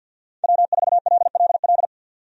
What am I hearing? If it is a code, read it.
Morse code, heard as a keyed tone, about 52 wpm, 700 Hz. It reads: W4LLL
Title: USA Call Signs - 5 Characters - 50wpm